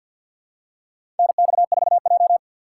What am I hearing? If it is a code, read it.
Morse code, heard as a keyed tone, about 52 wpm, 700 Hz. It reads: NX4J
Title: USA Call Signs - 4 Characters - 3x Character Spacing - 50wpm